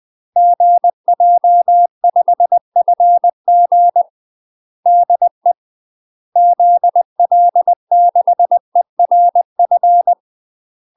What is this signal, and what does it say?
Morse code, heard as a keyed tone, about 20 wpm, 700 Hz. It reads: GJ5FG DE ZL6ERF
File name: UK MASTER QSOs TRAINING/20WPM